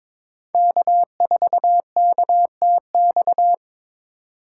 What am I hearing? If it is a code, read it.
Morse code, heard as a keyed tone, about 23 wpm, 700 Hz. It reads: K4KTX
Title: USA Call Signs - 6x Character Spacing - 22wpm